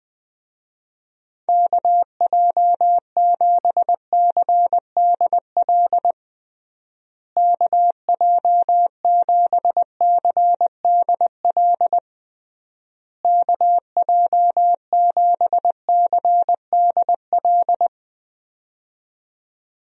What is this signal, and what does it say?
Morse code, heard as a keyed tone, about 21 wpm, 700 Hz. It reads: KJ7CDL KJ7CDL KJ7CDL
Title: USA Call Signs - 6 Characters - Repeated 3x - 20wpm